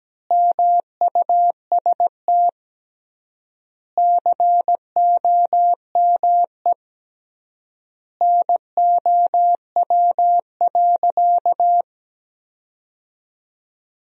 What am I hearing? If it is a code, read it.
Morse code, heard as a keyed tone, about 17 wpm, 700 Hz. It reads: MUST COME NOW.
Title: 3x Word-Spacing - Sentences from Top 200 Words - 17wpm